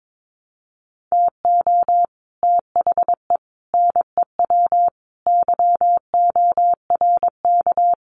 Morse code, heard as a keyed tone, about 22 wpm, 700 Hz. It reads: TO THE NEW YORK
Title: Sets of 5 Words Encore2 22wpm